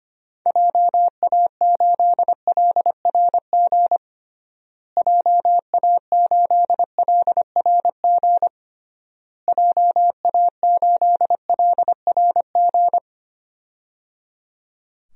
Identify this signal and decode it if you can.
Morse code, heard as a keyed tone, about 25 wpm, 700 Hz. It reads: JA8LRG JA8LRG JA8LRG
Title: DXCC-100 Call Signs - 6 Characters - Repeated 3x - 25wpm